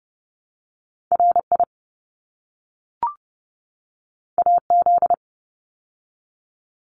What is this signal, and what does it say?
Morse code, heard as a keyed tone, about 30 wpm, 700 Hz. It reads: RI AZ
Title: ARRL Sections 30wpm